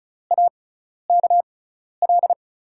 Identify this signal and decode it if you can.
Morse code, heard as a keyed tone, about 36 wpm, spaced as for 12 wpm, 700 Hz. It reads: AKL
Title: USA Call Signs - 5 Characters - 6x Character Spacing - 35wpm